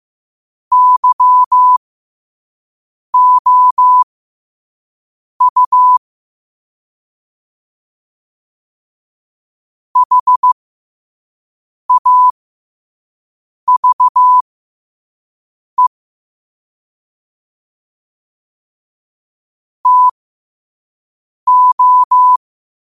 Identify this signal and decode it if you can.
Morse code, heard as a keyed tone, about 15 wpm, spaced as for 5 wpm, 1000 Hz. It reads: YOU HAVE TO
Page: New quotes every day in morse code at 5 Words per minute.